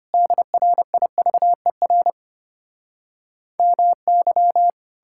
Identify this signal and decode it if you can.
Morse code, heard as a keyed tone, about 33 wpm, 700 Hz. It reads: DRIVER MY